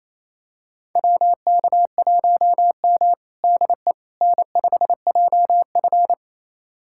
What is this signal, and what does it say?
Morse code, heard as a keyed tone, about 28 wpm, 700 Hz. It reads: WK1M DE N5JF
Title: USA Callsign DE Callsign - 4 Characters - 28wpm